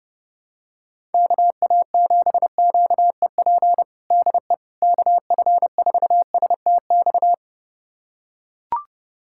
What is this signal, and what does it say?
Morse code, heard as a keyed tone, about 30 wpm, 700 Hz. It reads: KA7QEP DE KF4STX
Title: USA Callsign DE Callsign 30wpm